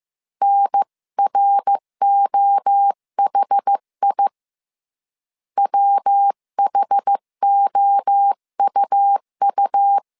Morse code, heard as a keyed tone, about 15 wpm, 800 Hz. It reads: NROHI WHOUU
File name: CW / DET / 1 3 1500C S A 14wpm 2447